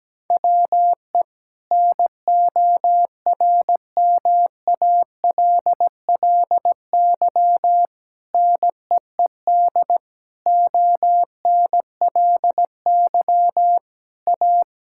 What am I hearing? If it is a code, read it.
Morse code, heard as a keyed tone, about 17 wpm, 700 Hz. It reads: WE NORMALLY NEED ONLY A
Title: Sentences from Top 1000 Words - Review Part 6 - 17wpm